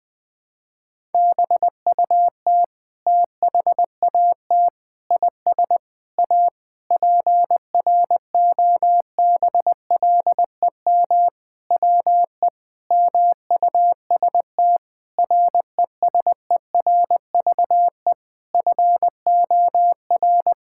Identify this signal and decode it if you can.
Morse code, heard as a keyed tone, about 21 wpm, 700 Hz. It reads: BUT THAT IS A PROBLEM WE MUST RESERVE FOR
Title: Sentences from Top 1000 Words - Review Part 6 - 20wpm